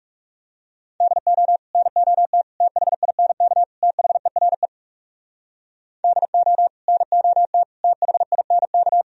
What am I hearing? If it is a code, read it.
Morse code, heard as a keyed tone, about 40 wpm, 700 Hz. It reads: DO NOT THINK THERE DO NOT THINK